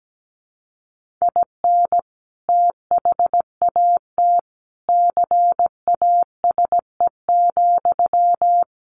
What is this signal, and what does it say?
Morse code, heard as a keyed tone, about 17 wpm, 700 Hz. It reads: IN THAT CASE,
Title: Sentences from Top 1400 Words 17wpm